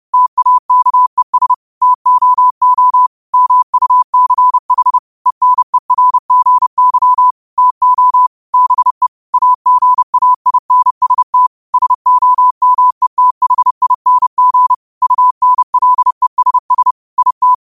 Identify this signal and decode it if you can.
Morse code, heard as a keyed tone, about 30 wpm, 1000 Hz. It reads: TAKES TOO MUCH ENERGY TO BE AGAINST SOMETHING UNLESS IT
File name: qotdfiles / 30wpm